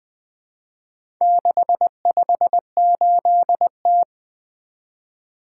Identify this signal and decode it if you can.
Morse code, heard as a keyed tone, about 20 wpm, 700 Hz. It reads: 658T
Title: Four Letter-Number Combinations - 20wpm